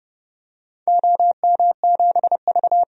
Morse code, heard as a keyed tone, about 30 wpm, 700 Hz. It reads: OM7V
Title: DXCC-100 Call Signs - 4 Characters - 30wpm